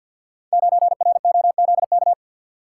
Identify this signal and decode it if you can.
Morse code, heard as a keyed tone, about 46 wpm, 700 Hz. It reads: 9ROZK
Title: Code Groups - 5 Letter:Numbers - 50wpm